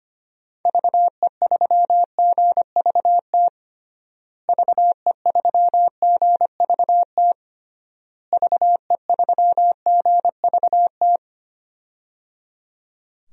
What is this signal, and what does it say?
Morse code, heard as a keyed tone, about 25 wpm, 700 Hz. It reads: VE3GVT VE3GVT VE3GVT
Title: DXCC-100 Call Signs - 6 Characters - Repeated 3x - 25wpm